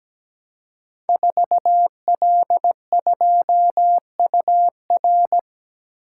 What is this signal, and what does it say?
Morse code, heard as a keyed tone, about 17 wpm, 700 Hz. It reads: 4L2UR
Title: Code Groups - 5 Letter:Numbers - 17wpm